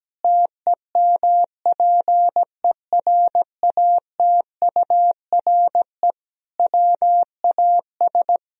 Morse code, heard as a keyed tone, about 17 wpm, 700 Hz. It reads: TEMPERATURE WAS
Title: Sentences from Top 1000 Words - Review Part 4 - 17wpm